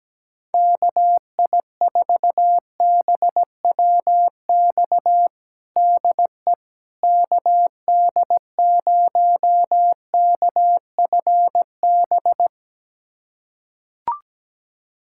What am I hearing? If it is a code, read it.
Morse code, heard as a keyed tone, about 17 wpm, 700 Hz. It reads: KI4BWX DE KD0KFB
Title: USA Callsign DE Callsign - 6 Characters - 17wpm